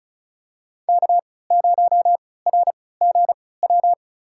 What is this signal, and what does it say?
Morse code, heard as a keyed tone, about 36 wpm, spaced as for 20 wpm, 700 Hz. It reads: K0RGW
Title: USA Call Signs - 3x Character Spacing - 35wpm